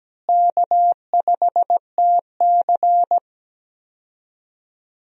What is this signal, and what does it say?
Morse code, heard as a keyed tone, about 17 wpm, 700 Hz. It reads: K5TC
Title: USA Call Signs - 4 Characters - Repeated 2x - 17wpm